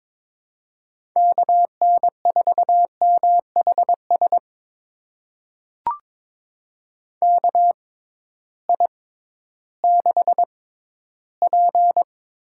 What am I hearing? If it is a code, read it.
Morse code, heard as a keyed tone, about 22 wpm, spaced as for 13 wpm, 700 Hz. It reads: KN4MHS KI6P
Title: USA Call Signs - 6 Characters - 6x Character Spacing - 22wpm